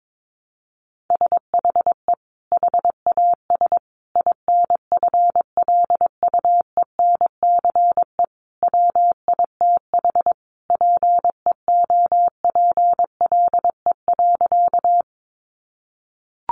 Morse code, heard as a keyed tone, about 22 wpm, 700 Hz. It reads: SHE HAS INFLUENCE WITH PEOPLE.
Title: Sentences from Top 1100 Words 22wpm